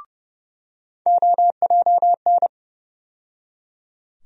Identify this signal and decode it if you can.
Morse code, heard as a keyed tone, about 29 wpm, 700 Hz. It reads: OJN
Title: USA Call Sign Postfixes - 3 Letters 30wpm